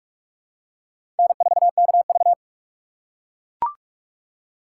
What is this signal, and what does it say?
Morse code, heard as a keyed tone, about 43 wpm, 700 Hz. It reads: N4KV
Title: USA Call Signs - 4 Characters - 6x Character Spacing - 45wpm